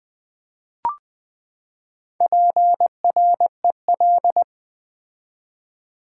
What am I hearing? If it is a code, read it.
Morse code, heard as a keyed tone, about 20 wpm, 700 Hz. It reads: PREL
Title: Build A Word - 4 Letters At A Time - 11-Letter Words - 20wpm